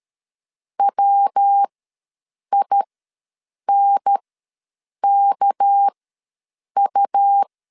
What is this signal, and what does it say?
Morse code, heard as a keyed tone, about 13 wpm, spaced as for 7 wpm, 800 Hz. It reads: WINKU
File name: LEZIONI / 6 PKX0 / 6 PK ws 12wpm 3116